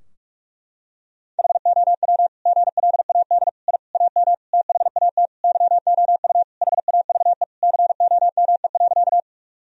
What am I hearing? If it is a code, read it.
Morse code, heard as a keyed tone, about 46 wpm, 700 Hz. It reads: HOW GLAD I AM THAT YOU HAVE COME.
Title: Sentences from Top 1000 Words - Review Part 2 - 45wpm